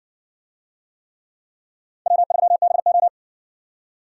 Morse code, heard as a keyed tone, about 60 wpm, 700 Hz. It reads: W3BO